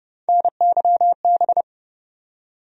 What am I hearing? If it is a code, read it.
Morse code, heard as a keyed tone, about 30 wpm, 700 Hz. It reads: NYB
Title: USA Call Sign Postfixes - 3 Letters 30wpm